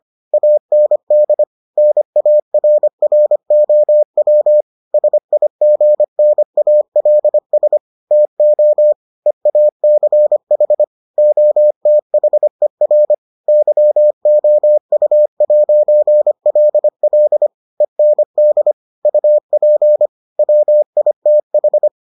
Morse code, heard as a keyed tone, about 25 wpm, 600 Hz. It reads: AND NARROW SIGNALS TO EACH OTHER YOU'LL END UP WITH